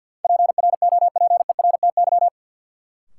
Morse code, heard as a keyed tone, about 52 wpm, 700 Hz. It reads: PROPERTY
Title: Sentences from Top 1000 Words - New Word Drill - 50wpm